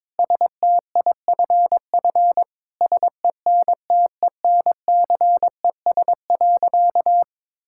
Morse code, heard as a keyed tone, about 22 wpm, 700 Hz. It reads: STIFF SENTENCES.
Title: Pangram Challenge v2 - 22wpm